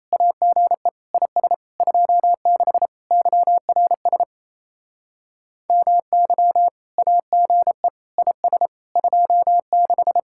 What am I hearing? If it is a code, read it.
Morse code, heard as a keyed tone, about 34 wpm, 700 Hz. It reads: AGE IS 26 YRS MY AGE IS 26
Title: Speed-Racing Common QSO Phrases - 28wpm